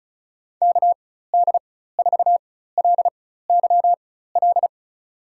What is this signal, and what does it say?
Morse code, heard as a keyed tone, about 36 wpm, spaced as for 16 wpm, 700 Hz. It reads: KD4LYL
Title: USA Call Signs - 6 Characters - 4x Character Spacing - 35wpm